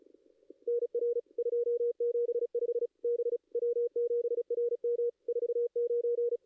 Morse code, heard as a keyed tone, about 36 wpm, 1430 Hz, with noise. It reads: NR 275 B W7RM 49
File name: contests / ss-cw-k5tr-op-n5rz -2300